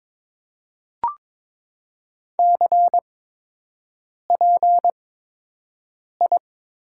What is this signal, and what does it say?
Morse code, heard as a keyed tone, about 22 wpm, 700 Hz. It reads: C P I
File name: Top 90 QSO Elements - 8x Character Spacing - 22wpm